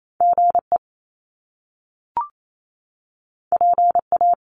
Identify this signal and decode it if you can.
Morse code, heard as a keyed tone, about 28 wpm, 700 Hz. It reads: GE PA
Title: Top Two Letter Combinations in Words 28wpm